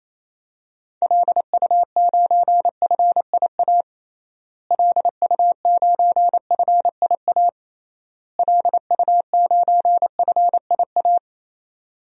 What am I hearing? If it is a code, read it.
Morse code, heard as a keyed tone, about 28 wpm, 700 Hz. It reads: LU9FIA LU9FIA LU9FIA
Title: DXCC-100 Call Signs - 6 Characters - Repeated 3x - 28wpm